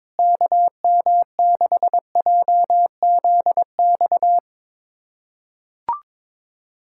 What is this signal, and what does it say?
Morse code, heard as a keyed tone, about 22 wpm, 700 Hz. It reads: KM6JZX
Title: USA Callsign DE Callsign - 6 Characters - 22wpm